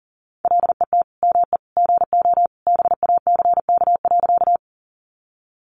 Morse code, heard as a keyed tone, about 40 wpm, 700 Hz. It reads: LET ME GO BACK.
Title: Sentences from Top 300 Words 40wpm